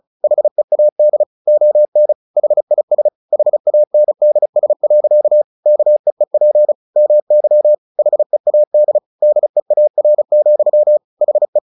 Morse code, heard as a keyed tone, about 34 wpm, 600 Hz. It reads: HEAD ON HIS HANDS. KEEP MY HEAD DEAR, HE
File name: the old man and the sea 2 35wpm